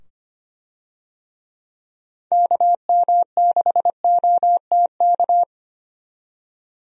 Morse code, heard as a keyed tone, about 25 wpm, 700 Hz. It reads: KM6OTK
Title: USA Call Signs - 6 Characters - 25wpm